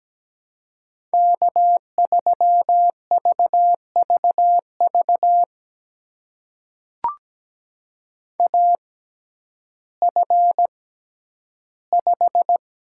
Morse code, heard as a keyed tone, about 17 wpm, spaced as for 10 wpm, 700 Hz. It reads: K3VVV AF5
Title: USA Call Signs - 5 Characters - 6x Character Spacing - 17wpm